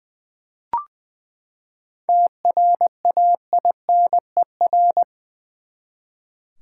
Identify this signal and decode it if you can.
Morse code, heard as a keyed tone, about 20 wpm, 700 Hz. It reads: TRAINER
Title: Seven Letter Words - 20wpm